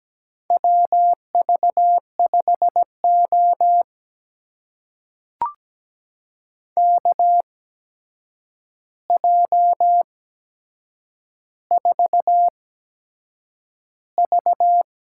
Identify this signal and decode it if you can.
Morse code, heard as a keyed tone, about 17 wpm, spaced as for 7 wpm, 700 Hz. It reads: WV5O KJ4V
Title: USA Call Signs - 8x Character Spacing - 17wpm